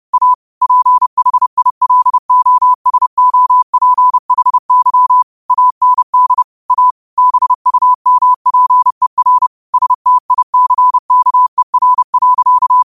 Morse code, heard as a keyed tone, about 30 wpm, 1000 Hz. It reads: A PHILOSOPHY AND A BUMPER STICKER.